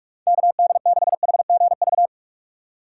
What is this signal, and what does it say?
Morse code, heard as a keyed tone, about 46 wpm, 700 Hz. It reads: KD6HGV
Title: USA Call Signs - 6 Characters - 3x Character Spacing - 45wpm